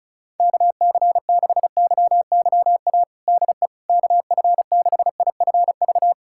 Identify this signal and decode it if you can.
Morse code, heard as a keyed tone, about 36 wpm, 700 Hz. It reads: KC6YYA DE KF6IFV
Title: USA Callsign DE Callsign - 6 Characters - 35wpm